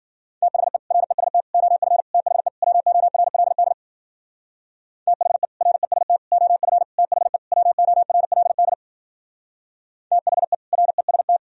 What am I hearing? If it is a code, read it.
Morse code, heard as a keyed tone, about 51 wpm, 700 Hz. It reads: THE REST OF THE WORLD THE REST OF THE WORLD THE REST